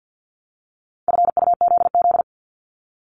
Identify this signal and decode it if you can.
Morse code, heard as a keyed tone, about 50 wpm, 700 Hz. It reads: FUZZ
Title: Multi-letter Phonemes 50wpm